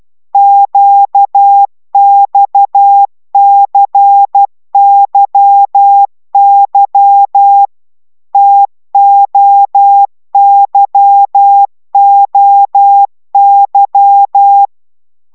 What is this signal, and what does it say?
Morse code, heard as a keyed tone, about 12 wpm, 800 Hz. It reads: QXCYY TOYOY